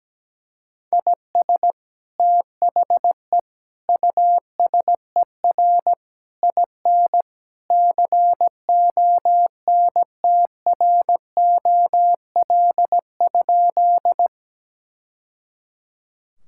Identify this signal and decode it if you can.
Morse code, heard as a keyed tone, about 17 wpm, 700 Hz. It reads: IS THE USER IN CONTROL?
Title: Sentences from Top 1000 Words - Review Part 5 - 17wpm